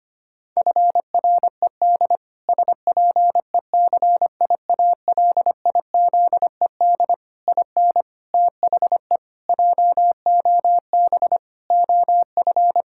Morse code, heard as a keyed tone, about 25 wpm, 700 Hz. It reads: FRED SPECIALIZED IN THE JOB OF
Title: Pangram Challenge v2 - 25wpm